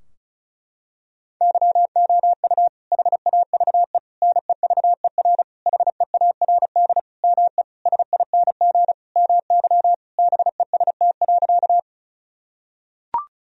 Morse code, heard as a keyed tone, about 36 wpm, 700 Hz. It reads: YOU HAVE NEVER HEARD ME SING MY BEST.
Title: Sentences from Top 1000 Words - Review Part 2 - 35wpm